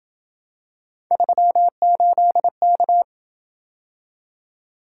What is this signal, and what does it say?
Morse code, heard as a keyed tone, about 26 wpm, 700 Hz. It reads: 38K